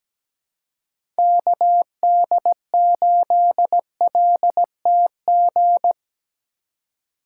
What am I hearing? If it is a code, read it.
Morse code, heard as a keyed tone, about 20 wpm, 700 Hz. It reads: KD8LTG
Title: Speed Racing - USA 2x3 Call Signs 17wpm